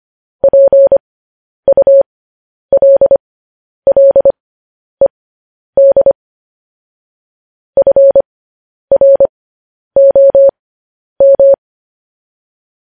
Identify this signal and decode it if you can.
Morse code, heard as a keyed tone, about 25 wpm, spaced as for 10 wpm, 550 Hz. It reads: PULLED FROM
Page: news-headlines-in-morse-code-at-10-wpm